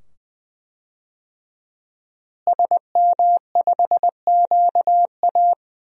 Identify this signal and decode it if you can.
Morse code, heard as a keyed tone, about 20 wpm, 700 Hz. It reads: SM5QA
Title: DXCC-100 Call Signs - 5 Characters - Repeated 2x - 20wpm